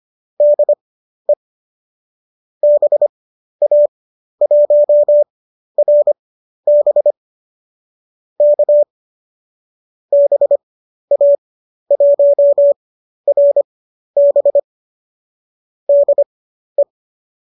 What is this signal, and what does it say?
Morse code, heard as a keyed tone, about 25 wpm, spaced as for 12 wpm, 590 Hz. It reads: DE BA1RB K BA1RB DE
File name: qso002 12